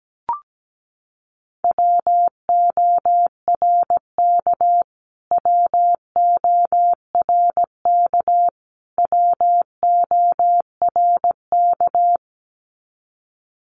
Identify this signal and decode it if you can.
Morse code, heard as a keyed tone, about 17 wpm, 700 Hz. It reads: WORK WORK WORK
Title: Top 300 Words Repeated 17wpm